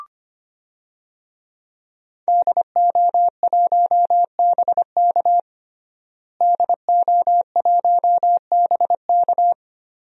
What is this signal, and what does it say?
Morse code, heard as a keyed tone, about 25 wpm, 700 Hz. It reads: DO1BK DO1BK
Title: DXCC-100 Call Signs - 5 Characters - Repeated 3x - 25wpm